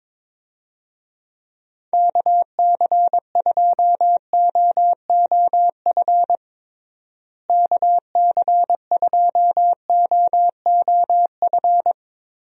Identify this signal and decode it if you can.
Morse code, heard as a keyed tone, about 22 wpm, 700 Hz. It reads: KC2OOF KC2OOF
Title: USA Call Signs - 6 Characters - Repeated 3x - 22wpm